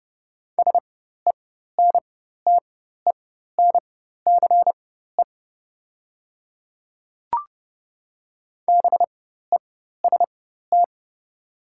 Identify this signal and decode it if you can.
Morse code, heard as a keyed tone, about 30 wpm, spaced as for 14 wpm, 690 Hz. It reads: SENTENCE BEST
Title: 4x Character Spacing - Top 500 Words - 30wpm